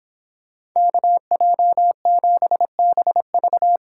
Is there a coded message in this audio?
Yes: Morse code from 0.8 to 3.8 s, heard as a keyed tone, about 26 wpm, 700 Hz. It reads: KJ7BV